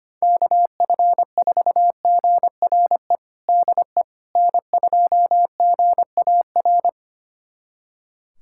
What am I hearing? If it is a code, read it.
Morse code, heard as a keyed tone, about 25 wpm, 700 Hz. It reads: KF4GRE DE N2GAR
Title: USA Callsign DE Callsign 25wpm